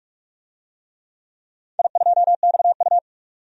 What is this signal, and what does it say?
Morse code, heard as a keyed tone, about 46 wpm, 700 Hz. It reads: I2XU
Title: DXCC-100 Call Signs - 4 Characters - 45wpm